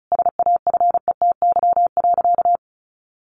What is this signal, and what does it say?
Morse code, heard as a keyed tone, about 34 wpm, 700 Hz. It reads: SAFETY.
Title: Sentences from Top 1000 Words 35wpm